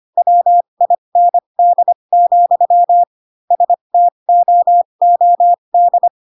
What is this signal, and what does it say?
Morse code, heard as a keyed tone, about 25 wpm, spaced as for 22 wpm, 700 Hz. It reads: WIND, STOOD
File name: Pinocchio / Pinocchio ch3 18 25 ik1hge